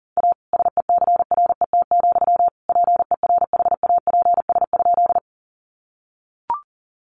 Morse code, heard as a keyed tone, about 40 wpm, 700 Hz. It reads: A SECRET, PERHAPS?
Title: Sentences from Top 1200 Words 40wpm